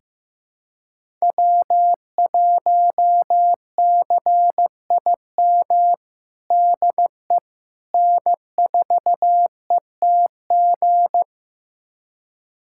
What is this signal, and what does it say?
Morse code, heard as a keyed tone, about 16 wpm, 700 Hz. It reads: W1CIM DE N4ETG
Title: USA Callsign DE Callsign - 5 Characters - 15wpm